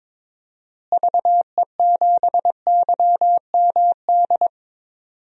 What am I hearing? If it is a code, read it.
Morse code, heard as a keyed tone, about 22 wpm, 700 Hz. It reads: VE7YMD
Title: DXCC-100 Call Signs - 6 Characters - Repeated 2x - 22wpm